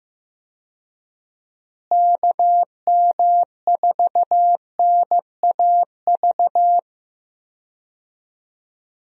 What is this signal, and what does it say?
Morse code, heard as a keyed tone, about 15 wpm, 700 Hz. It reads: KM4NAV
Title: USA Call Signs - 6 Characters - Repeated 3x - 15wpm